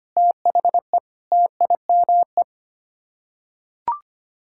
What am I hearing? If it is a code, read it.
Morse code, heard as a keyed tone, about 25 wpm, 700 Hz. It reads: THE TIME
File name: Sets of 9 Words - 25wpm